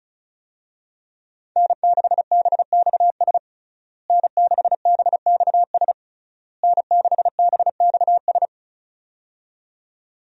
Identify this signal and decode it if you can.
Morse code, heard as a keyed tone, about 36 wpm, 700 Hz. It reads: N6BXS N6BXS N6BXS
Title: USA Call Signs - 5 Characters - Repeated 3x - 35wpm